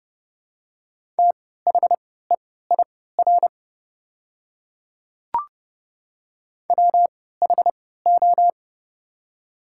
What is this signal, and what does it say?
Morse code, heard as a keyed tone, about 30 wpm, spaced as for 17 wpm, 700 Hz. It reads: THEIR WHO
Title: 3x Character Spacing - Top 100 Words - 30wpm